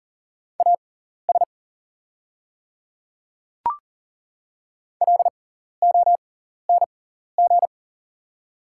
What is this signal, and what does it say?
Morse code, heard as a keyed tone, about 40 wpm, spaced as for 14 wpm, 690 Hz. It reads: AS LONG
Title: 6x Character Spacing - Top 100 Words - 40wpm